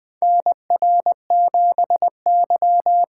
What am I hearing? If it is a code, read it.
Morse code, heard as a keyed tone, about 20 wpm, 700 Hz. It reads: NR7Y
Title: 2023 SOTA Chaser Call Signs - 20wpm